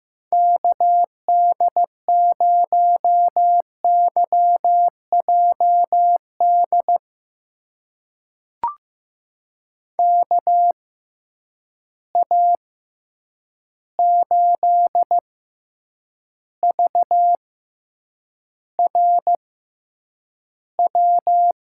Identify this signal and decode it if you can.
Morse code, heard as a keyed tone, about 15 wpm, spaced as for 8 wpm, 700 Hz. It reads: KD0YJD KA8VRW
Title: USA Call Signs - 6x Character Spacing - 15wpm